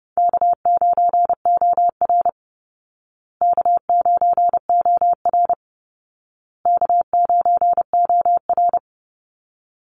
Morse code, heard as a keyed tone, about 28 wpm, 700 Hz. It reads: K9OR K9OR K9OR
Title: Call Signs Repeated 2x - 30wpm